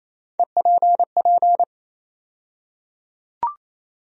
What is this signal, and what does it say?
Morse code, heard as a keyed tone, about 28 wpm, 700 Hz. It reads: EPP
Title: USA Call Sign Postfixes - 3 Letters 28wpm